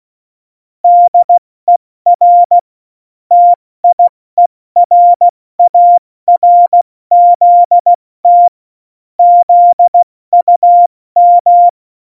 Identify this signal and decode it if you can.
Morse code, heard as a keyed tone, about 16 wpm, 700 Hz. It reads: DER TIERARZT ZUM
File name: Morsekurs / Witze / Witz 25. Dezember, 70 F80